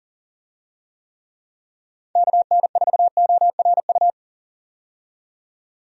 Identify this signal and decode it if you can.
Morse code, heard as a keyed tone, about 40 wpm, 700 Hz. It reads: KN4ORU
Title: USA Call Signs - 6 Characters - 40wpm